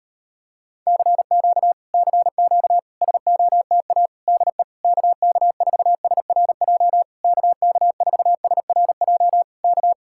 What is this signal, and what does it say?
Morse code, heard as a keyed tone, about 34 wpm, 700 Hz. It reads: CQ CQ SOTA DE KK4SRJ KK4SRJ K